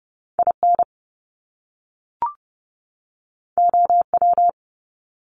Morse code, heard as a keyed tone, about 30 wpm, 700 Hz. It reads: IN OW
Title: Top Two Letter Combinations in Words 30wpm